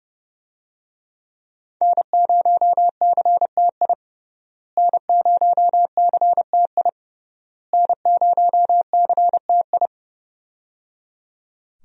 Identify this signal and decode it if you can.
Morse code, heard as a keyed tone, about 30 wpm, 700 Hz. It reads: N0CTI N0CTI N0CTI
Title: USA Call Signs - 5 Characters - Repeated 3x - 30wpm